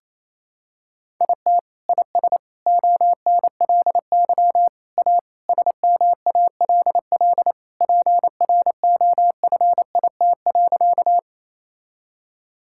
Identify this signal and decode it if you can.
Morse code, heard as a keyed tone, about 28 wpm, 700 Hz. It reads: IT IS ONLY A SMALL PROFIT.
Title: Sentences from Top 1000 Words - Review Part 5 - 28wpm